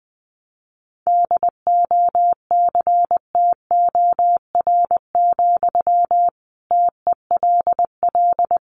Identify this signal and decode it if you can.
Morse code, heard as a keyed tone, about 20 wpm, 700 Hz. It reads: DOCTOR, TELL
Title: Sentences from Top 1500 Words 20wpm